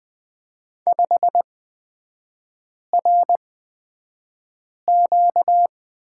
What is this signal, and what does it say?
Morse code, heard as a keyed tone, about 20 wpm, spaced as for 5 wpm, 700 Hz. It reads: 5RQ
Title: 236 Lesson 38e - Letter q - Three Characters 20wpm